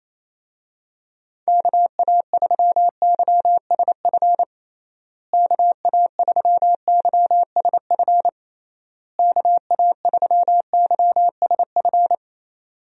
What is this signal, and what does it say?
Morse code, heard as a keyed tone, about 28 wpm, 700 Hz. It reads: KA3YSF KA3YSF KA3YSF
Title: USA Call Signs - 6 Characters - Repeated 3x - 28wpm